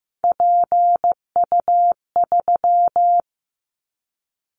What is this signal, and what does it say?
Morse code, heard as a keyed tone, about 15 wpm, 700 Hz. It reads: PU3
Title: Callsign Prefixes for Top 100 DXCC - 15wpm